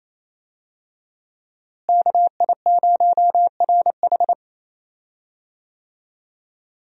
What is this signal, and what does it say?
Morse code, heard as a keyed tone, about 28 wpm, 700 Hz. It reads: KI0RH
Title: USA Call Signs - 5 Characters - Repeated 2x - 28wpm